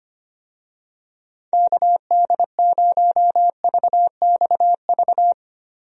Morse code, heard as a keyed tone, about 25 wpm, 700 Hz. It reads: KD0VXV
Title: USA Call Signs - 6 Characters - Repeated 3x - 25wpm